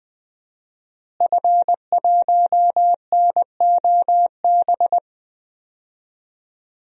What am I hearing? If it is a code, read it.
Morse code, heard as a keyed tone, about 20 wpm, 700 Hz. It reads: F1NOB
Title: DXCC-100 Call Signs - 5 Characters - Repeated 3x - 20wpm